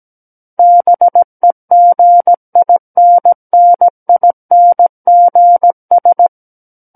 Morse code, heard as a keyed tone, about 17 wpm, 700 Hz. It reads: BEGINNINGS
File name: Book One / 08-Chapter 8 - 17 WPM